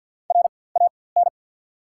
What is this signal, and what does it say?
Morse code, heard as a keyed tone, about 52 wpm, spaced as for 24 wpm, 700 Hz. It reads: RAN
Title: 4x Character Spacing - Top 500 Words - 50wpm